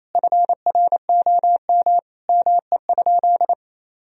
Morse code, heard as a keyed tone, about 28 wpm, 700 Hz. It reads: FROM ME?
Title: Sentences from Top 1000 Words - Review Part 4 - 28wpm